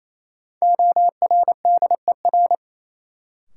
Morse code, heard as a keyed tone, about 28 wpm, 700 Hz. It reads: ORDER
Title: Sets of 2 Words - New Word Drill - 28wpm